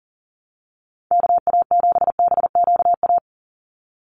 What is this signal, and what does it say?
Morse code, heard as a keyed tone, about 40 wpm, 700 Hz. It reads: KA7BQA
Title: USA Call Signs 40wpm